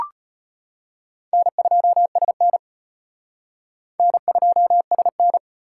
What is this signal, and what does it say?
Morse code, heard as a keyed tone, about 38 wpm, 700 Hz. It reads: N2SN N2SN